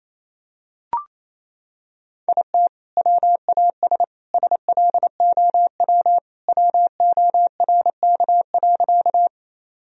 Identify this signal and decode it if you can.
Morse code, heard as a keyed tone, about 27 wpm, 700 Hz. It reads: IT WAS SLOW WORK.
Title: Sentences from Top 1000 Words - Review Part 2 - 28wpm